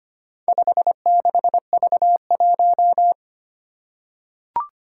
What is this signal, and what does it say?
Morse code, heard as a keyed tone, about 25 wpm, 700 Hz. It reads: 56V1
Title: Four Challenging Letter-Number Combinations 25wpm